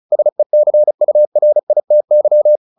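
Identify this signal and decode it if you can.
Morse code, heard as a keyed tone, about 34 wpm, 600 Hz. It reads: SECURITY
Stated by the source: Morse practice at 35 wpm from CW med Gnister, sending Ken Follett's novel Never in English.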